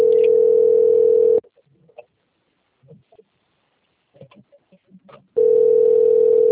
Ringback tone (North America), 440 + 480 Hz.
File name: out- - - - - .36510